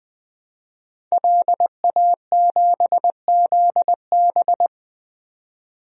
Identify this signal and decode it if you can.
Morse code, heard as a keyed tone, about 21 wpm, 700 Hz. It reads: LA7ZB
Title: DXCC-100 Call Signs - 5 Characters - 20wpm